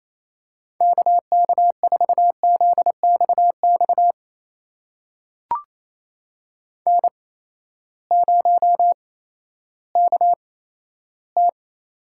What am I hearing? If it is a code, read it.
Morse code, heard as a keyed tone, about 28 wpm, spaced as for 14 wpm, 700 Hz. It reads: KK4ZXX N0KT
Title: USA Call Signs - 8x Character Spacing - 28wpm